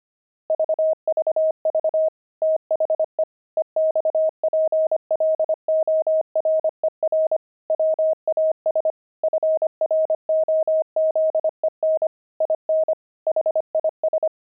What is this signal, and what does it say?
Morse code, heard as a keyed tone, about 25 wpm, 630 Hz. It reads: VVV THE EXPLORER WAS FROZEN IN HIS
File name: pangram-fs-25-cs-25-explorer